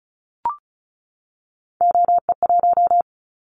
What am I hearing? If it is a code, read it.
Morse code, heard as a keyed tone, about 35 wpm, 700 Hz. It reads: OE1
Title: Callsign Prefixes for Top 100 DXCC - 35wpm